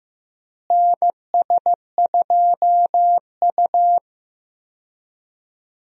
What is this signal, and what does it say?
Morse code, heard as a keyed tone, about 15 wpm, 700 Hz. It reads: NS2U
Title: USA Call Signs - 4 Characters - Repeated 3x - 15wpm